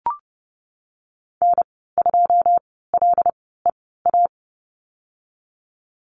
Morse code, heard as a keyed tone, about 30 wpm, spaced as for 17 wpm, 700 Hz. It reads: N2LEA
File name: Lesson 19g - Letter w - Call Signs 30wpm